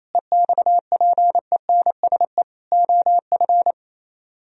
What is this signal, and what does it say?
Morse code, heard as a keyed tone, about 36 wpm, 700 Hz. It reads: EXPENSE OF